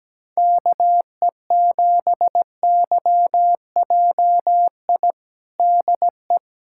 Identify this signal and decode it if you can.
Morse code, heard as a keyed tone, about 17 wpm, 700 Hz. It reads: KE7YJI DE
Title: USA Callsign DE Callsign - 6 Characters - 17wpm